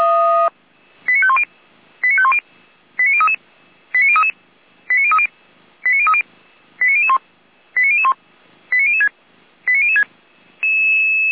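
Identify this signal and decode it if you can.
Five-tone selective call (ZVEI), heard as quick runs of five tones, tones of about 70 ms. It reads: 89510 89510 89033 89033 89030 89030 89001 89001 89006 89006
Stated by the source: ZVEI five-tone selective call received on an SDR.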